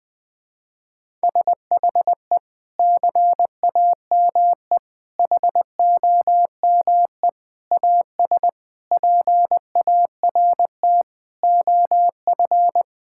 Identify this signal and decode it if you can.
Morse code, heard as a keyed tone, about 20 wpm, 700 Hz. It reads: SHE CAME HOME AS PART OF
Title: Sentences with 4-Word Phrases - 20wpm